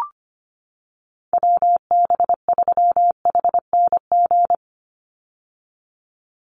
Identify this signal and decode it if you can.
Morse code, heard as a keyed tone, about 25 wpm, 700 Hz. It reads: WB3HNG
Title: USA Call Signs 25wpm